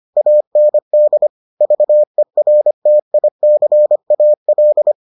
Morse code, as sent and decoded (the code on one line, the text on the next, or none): .- -. -.. / ...- . .-. - .. -.-. .- .-..
AND VERTICAL